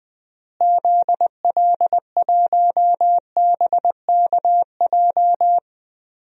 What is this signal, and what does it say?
Morse code, heard as a keyed tone, about 20 wpm, 700 Hz. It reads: ZL1BKJ
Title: DXCC-100 Call Signs - 6 Characters - Repeated 2x - 20wpm